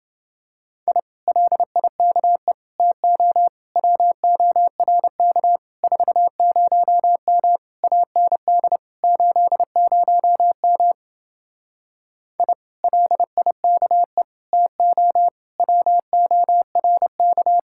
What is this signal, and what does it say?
Morse code, heard as a keyed tone, about 28 wpm, 700 Hz. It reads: I LIKE TO WORK 40M AND 80M I LIKE TO WORK